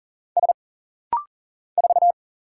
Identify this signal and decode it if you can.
Morse code, heard as a keyed tone, about 40 wpm, 700 Hz. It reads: S 4
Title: Single Letter-Number - Rapid-Fire 40wpm